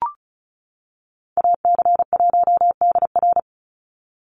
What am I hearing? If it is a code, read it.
Morse code, heard as a keyed tone, about 34 wpm, 700 Hz. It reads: AC1DR
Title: USA Call Signs 35wpm